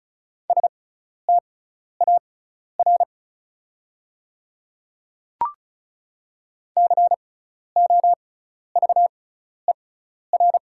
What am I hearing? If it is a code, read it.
Morse code, heard as a keyed tone, about 36 wpm, spaced as for 12 wpm, 690 Hz. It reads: STAR COVER
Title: 6x Character Spacing - Top 500 Words - 35wpm